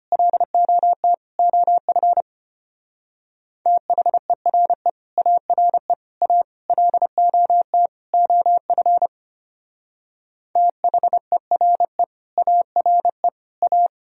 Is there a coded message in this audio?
Yes: Morse code from 0.1 to 13.9 s, heard as a keyed tone, about 32 wpm, 700 Hz. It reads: LOT OF THERE ARE A LOT OF THERE ARE A